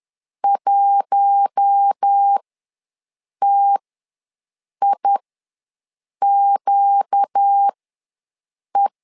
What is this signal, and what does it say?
Morse code, heard as a keyed tone, about 10 wpm, spaced as for 6 wpm, 800 Hz. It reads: 1TIQE
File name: LEZIONI / 4 Q1 ws 10wpm 2538